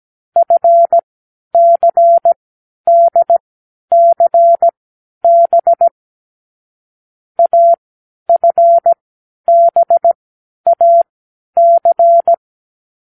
Morse code, heard as a keyed tone, about 17 wpm, spaced as for 10 wpm, 680 Hz. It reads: FCDCB AFBAC
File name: k7qo / Lesson9(A-F)